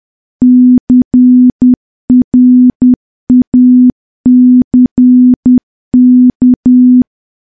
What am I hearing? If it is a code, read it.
Morse code, heard as a keyed tone, about 10 wpm, 250 Hz. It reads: CRACK
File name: morse / morse-training-10WPM